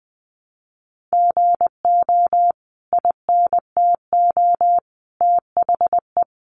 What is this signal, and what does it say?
Morse code, heard as a keyed tone, about 20 wpm, 700 Hz. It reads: GO INTO THE
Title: Sentences from Top 1300 Words 20wpm